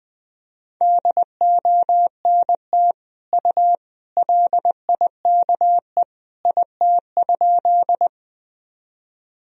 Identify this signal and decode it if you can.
Morse code, heard as a keyed tone, about 21 wpm, 700 Hz. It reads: DONT U LIKE IT?
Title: Sentences from Top 300 Words and Abbreviations 20wpm